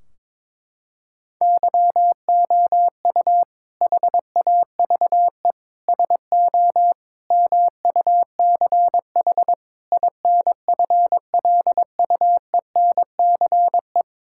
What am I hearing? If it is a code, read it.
Morse code, heard as a keyed tone, about 22 wpm, 700 Hz. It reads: YOU HAVE SO MUCH INFLUENCE
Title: Sentences from Top 1600 Words 22wpm